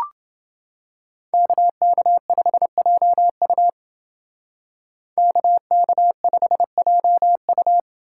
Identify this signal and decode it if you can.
Morse code, heard as a keyed tone, about 29 wpm, 700 Hz. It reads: KK5JU KK5JU